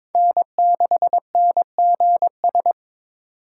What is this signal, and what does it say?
Morse code, heard as a keyed tone, about 23 wpm, 700 Hz. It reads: N6NGS
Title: USA Call Signs - 5 Characters - 6x Character Spacing - 22wpm